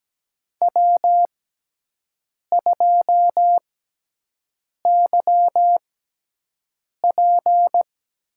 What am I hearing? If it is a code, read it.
Morse code, heard as a keyed tone, about 17 wpm, spaced as for 6 wpm, 700 Hz. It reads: W2YP
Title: USA Call Signs - 5 Characters - 6x Character Spacing - 17wpm